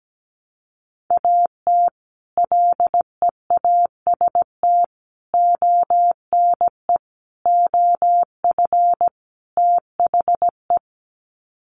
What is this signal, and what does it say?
Morse code, heard as a keyed tone, about 17 wpm, 700 Hz. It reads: AT LEAST ONE OF THE
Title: Sets of 5 Words Encore 17wpm